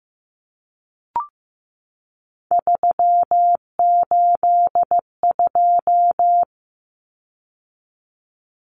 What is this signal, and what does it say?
Morse code, heard as a keyed tone, about 15 wpm, 700 Hz. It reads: 382
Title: Common Cut Numbers - 3-Digit 15wpm